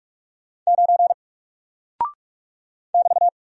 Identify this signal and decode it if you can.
Morse code, heard as a keyed tone, about 44 wpm, 700 Hz. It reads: 9 <BT>
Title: Single Character (Letter:Number:Punctuation:Prosign) Alt 45wpm